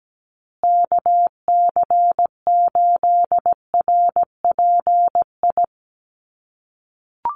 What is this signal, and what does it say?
Morse code, heard as a keyed tone, about 17 wpm, 700 Hz. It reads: KC8RPI
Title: USA Call Signs Repeated 2x 17wpm